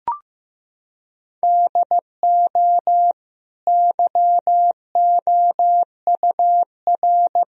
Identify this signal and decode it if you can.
Morse code, heard as a keyed tone, about 15 wpm, 700 Hz. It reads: DO YOUR
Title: Sentences from Top 1000 Words - Review Part 1 - 15wpm